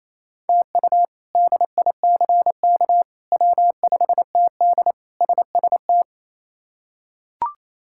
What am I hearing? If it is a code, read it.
Morse code, heard as a keyed tone, about 28 wpm, 700 Hz. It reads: TU DICK W5TD SST
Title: K1USN SST QSO Practice - 28wpm